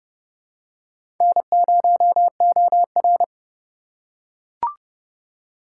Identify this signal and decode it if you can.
Morse code, heard as a keyed tone, about 32 wpm, 700 Hz. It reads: N0OR
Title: USA Call Signs - 2x Character Spacing - 30wpm